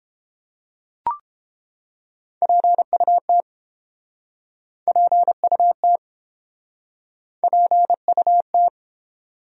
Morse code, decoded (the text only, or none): PUT PUT PUT